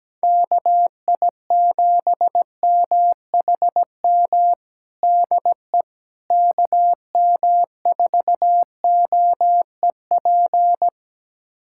Morse code, heard as a keyed tone, about 17 wpm, 700 Hz. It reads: KI7MHM DE KM4OEP
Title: USA Callsign DE Callsign - 6 Characters - 17wpm